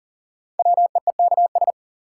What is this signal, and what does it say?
Morse code, heard as a keyed tone, about 40 wpm, 700 Hz. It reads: WEEKS
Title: Sets of 5 Words - New Word Drill - 40wpm